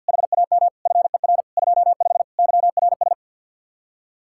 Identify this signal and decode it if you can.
Morse code, heard as a keyed tone, about 66 wpm, spaced as for 57 wpm, 700 Hz. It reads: HAM FER 25 YRS
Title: Speed-Racing Common QSO Phrases - 50wpm